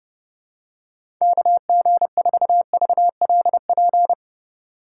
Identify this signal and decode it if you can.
Morse code, heard as a keyed tone, about 30 wpm, 700 Hz. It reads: KG4VLP
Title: USA Call Signs - 6 Characters - 2x Character Spacing - 30wpm